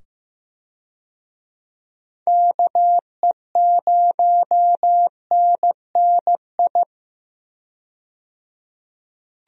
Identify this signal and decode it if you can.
Morse code, heard as a keyed tone, about 15 wpm, 700 Hz. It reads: KE0NNI
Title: USA Call Signs - 6 Characters - 15wpm